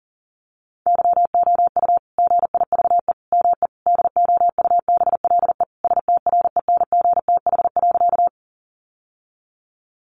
Morse code, heard as a keyed tone, about 40 wpm, 700 Hz. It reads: YOU GIVE ME DOUBLE STRENGTH.
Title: Sentences from Top 1400 Words 40wpm